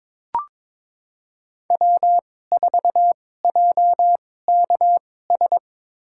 Morse code, heard as a keyed tone, about 22 wpm, spaced as for 16 wpm, 700 Hz. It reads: W4JKS
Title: USA Call Signs - 5 Characters - 2x Character Spacing - 22wpm